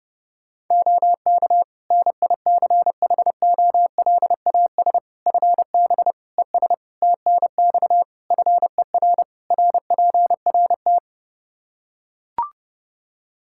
Morse code, heard as a keyed tone, about 30 wpm, 700 Hz. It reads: OK NICHOLAS FB ES TNX FER RPRT
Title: Practice QSO - Exchange 2 - Info - 30wpm